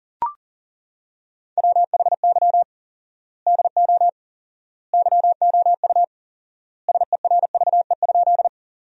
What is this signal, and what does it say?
Morse code, heard as a keyed tone, about 40 wpm, 700 Hz. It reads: WHY DO YOU SERVE?
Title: 4x Word-Spacing - Sentences from Top 400 Words - 40wpm